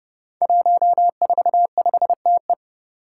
Morse code, heard as a keyed tone, about 28 wpm, 700 Hz. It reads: 145TE
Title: Five Challenging Letter-Number Combinations 30wpm